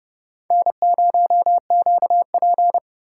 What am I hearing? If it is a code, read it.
Morse code, heard as a keyed tone, about 31 wpm, 700 Hz. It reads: N0QP
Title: USA Call Signs - 4 Characters - 4x Character Spacing - 30wpm